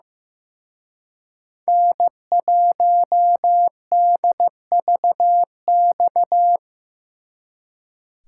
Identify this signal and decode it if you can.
Morse code, heard as a keyed tone, about 18 wpm, 700 Hz. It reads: N1DVX
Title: Speed Racing - USA 1x3 Call Signs 15wpm